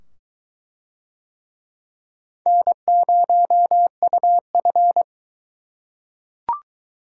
Morse code, heard as a keyed tone, about 22 wpm, 700 Hz. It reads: N0UF